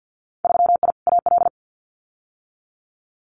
Morse code, heard as a keyed tone, about 50 wpm, 700 Hz. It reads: 3I AL
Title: Winter Field Day Exchanges Encore 50wpm